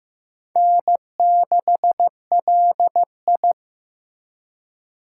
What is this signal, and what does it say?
Morse code, heard as a keyed tone, about 15 wpm, 700 Hz. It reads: N6LI
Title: USA Call Signs - 8x Character Spacing - 15wpm